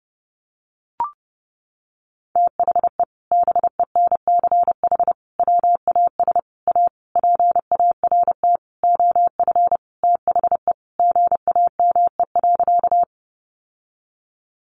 Morse code, heard as a keyed tone, about 28 wpm, 700 Hz. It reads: THE BENCH WAS A PART OF THE GAME.
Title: Sentences from Top 1400 Words 30wpm